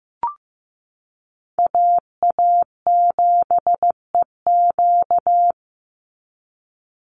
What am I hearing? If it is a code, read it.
Morse code, heard as a keyed tone, about 15 wpm, 700 Hz. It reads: AA7EQ
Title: NA SOTA Chaser Callsigns 15wpm